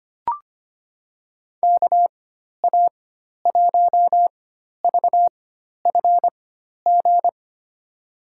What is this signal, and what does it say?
Morse code, heard as a keyed tone, about 25 wpm, spaced as for 12 wpm, 700 Hz. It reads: KA1VFG
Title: USA Call Signs - 6 Characters - 4x Character Spacing - 25wpm